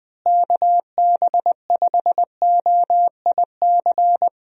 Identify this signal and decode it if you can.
Morse code, heard as a keyed tone, about 20 wpm, 700 Hz. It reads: KB5OIC
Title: USA Call Signs - 6 Characters - Repeated 2x - 20wpm